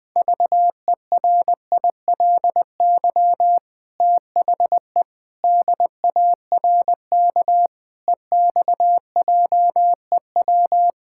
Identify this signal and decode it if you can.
Morse code, heard as a keyed tone, about 19 wpm, 700 Hz. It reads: VERILY THE DARK EXJEW
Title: Pangram Challenge v2 - 20wpm